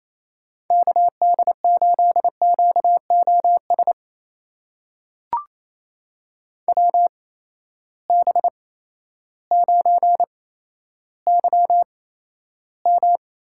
Morse code, heard as a keyed tone, about 28 wpm, spaced as for 13 wpm, 700 Hz. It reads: KD8QOS WB9YM
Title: USA Call Signs - 6 Characters - 8x Character Spacing - 28wpm